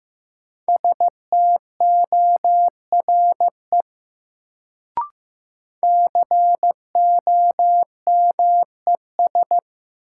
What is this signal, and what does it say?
Morse code, heard as a keyed tone, about 15 wpm, 700 Hz. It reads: STORE COMES
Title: Sentences from Top 1000 Words - Review Part 5 - New Word Drill - 15wpm